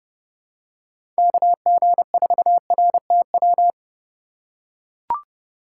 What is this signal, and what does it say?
Morse code, heard as a keyed tone, about 30 wpm, 700 Hz. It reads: KG4RTW
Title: USA Call Signs - 3x Character Spacing - 30wpm